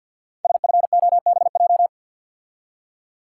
Morse code, heard as a keyed tone, about 50 wpm, 690 Hz. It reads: SFOBJ
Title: Code Groups - 5 Letters - 50wpm